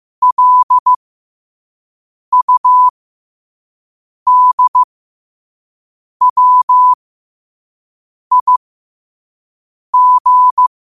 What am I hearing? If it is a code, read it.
Morse code, heard as a keyed tone, about 15 wpm, spaced as for 5 wpm, 1000 Hz. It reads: LUDWIG
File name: qotdfiles / 5wpm